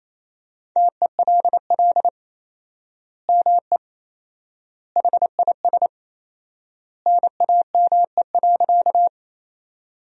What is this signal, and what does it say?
Morse code, heard as a keyed tone, about 28 wpm, 700 Hz. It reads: TELL ME HIS NAME.
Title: 4x Word-Spacing - Sentences from Top 200 Words - 28wpm